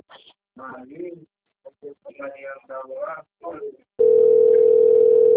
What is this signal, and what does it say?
Ringback tone (North America), 440 + 480 Hz, with speech.
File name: out- - - - - .27250